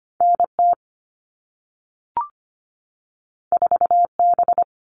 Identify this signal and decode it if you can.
Morse code, heard as a keyed tone, about 25 wpm, 700 Hz. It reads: NT 4B
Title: All Cut Numbers - 2-Digit 25wpm